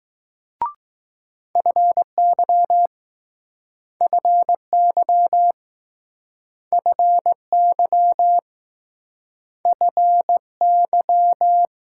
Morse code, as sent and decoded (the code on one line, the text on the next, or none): ..-. -.-- / ..-. -.-- / ..-. -.-- / ..-. -.--
FY FY FY FY